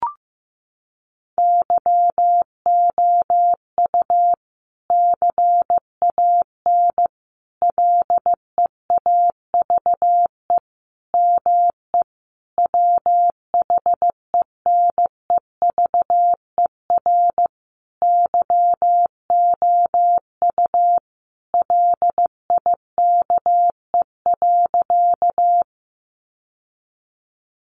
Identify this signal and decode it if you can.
Morse code, heard as a keyed tone, about 14 wpm, 700 Hz. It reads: YOU CAN LEAVE ME WHENEVER YOU LIKE.
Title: Sentences from Top 1200 Words 15wpm